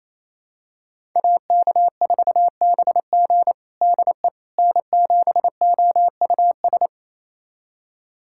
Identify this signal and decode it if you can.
Morse code, heard as a keyed tone, about 28 wpm, 700 Hz. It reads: AK4BG DE N7OUS
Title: USA Callsign DE Callsign - 5 Characters - 28wpm